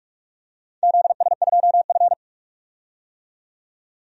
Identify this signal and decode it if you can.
Morse code, heard as a keyed tone, about 44 wpm, 700 Hz. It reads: ZS2F
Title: DXCC-100 Call Signs - 4 Characters - 45wpm